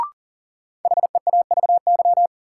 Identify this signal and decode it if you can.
Morse code, heard as a keyed tone, about 40 wpm, 700 Hz. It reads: HEAVY
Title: Sentences from Top 1100 Words - New Word Drill Alt - 40wpm